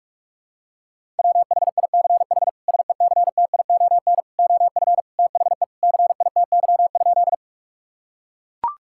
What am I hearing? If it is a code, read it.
Morse code, heard as a keyed tone, about 41 wpm, 700 Hz. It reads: WHICH SECTION OF THE CITY?
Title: Sentences from Top 1000 Words - Review Part 5 - 45wpm